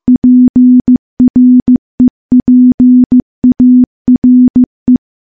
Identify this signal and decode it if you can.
Morse code, heard as a keyed tone, about 15 wpm, 250 Hz. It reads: PREPARE
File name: morse / morse-training-15WPM